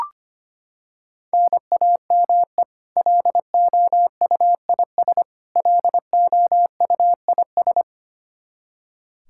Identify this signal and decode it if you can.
Morse code, heard as a keyed tone, about 25 wpm, 700 Hz. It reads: NAME LOUIS LOUIS
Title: K1USN SST QSO Practice 25wpm